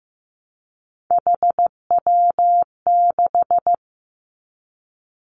Morse code, heard as a keyed tone, about 15 wpm, 700 Hz. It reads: HW6
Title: Three Challenging Letter-Number Combinations 15wpm